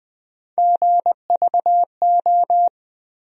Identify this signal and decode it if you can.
Morse code, heard as a keyed tone, about 20 wpm, 700 Hz. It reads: GVO
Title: USA Call Sign Postfixes - 3 Letters 20wpm